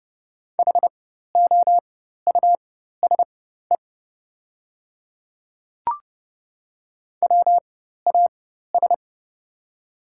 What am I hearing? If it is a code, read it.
Morse code, heard as a keyed tone, about 32 wpm, spaced as for 14 wpm, 700 Hz. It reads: HOUSE WAS
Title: 4x Character Spacing - Top 200 Words - 30wpm